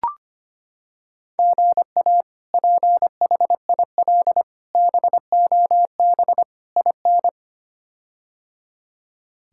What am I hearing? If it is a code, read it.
Morse code, heard as a keyed tone, about 25 wpm, 700 Hz. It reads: GA PHIL BOB IN
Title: K1USN SST QSO Practice - 25wpm